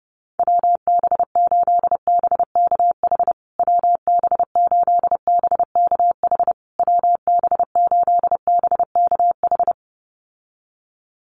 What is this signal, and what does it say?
Morse code, heard as a keyed tone, about 28 wpm, 700 Hz. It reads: WB8BKH WB8BKH WB8BKH
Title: USA Call Signs Repeated 3x 30wpm